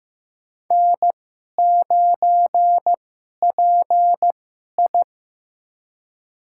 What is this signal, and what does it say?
Morse code, heard as a keyed tone, about 15 wpm, spaced as for 11 wpm, 700 Hz. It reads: N9PI
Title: USA Call Signs - 2x Character Spacing - 15wpm